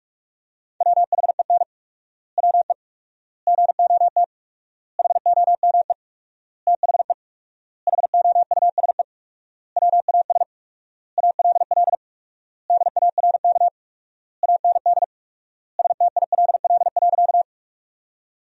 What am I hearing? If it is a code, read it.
Morse code, heard as a keyed tone, about 42 wpm, 690 Hz. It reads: WHEN WE GOT HOME THE HOUSE WAS ALL DARK AND STILL.
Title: 4x Word-Spacing - Sentences from Top 500 Words - 45wpm